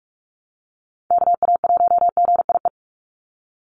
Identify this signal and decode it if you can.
Morse code, heard as a keyed tone, about 44 wpm, 700 Hz. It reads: KA1GIE
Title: USA Call Signs 45wpm